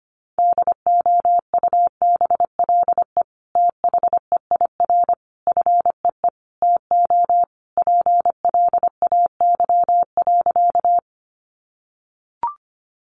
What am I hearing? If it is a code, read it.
Morse code, heard as a keyed tone, about 25 wpm, 700 Hz. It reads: DOUBLE THEIR FEE TO PLAY.
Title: Sentences from Top 1500 Words 25wpm